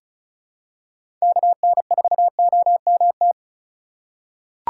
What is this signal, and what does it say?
Morse code, heard as a keyed tone, about 36 wpm, 700 Hz. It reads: KN4OMT
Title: USA Call Signs - 6 Characters - 4x Character Spacing - 35wpm